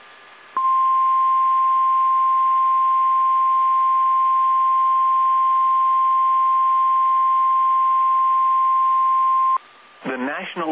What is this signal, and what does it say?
Emergency Alert System (SAME) tones (the 1050 Hz attention tone), with speech. It is heard from 0.5 to 9.6 s.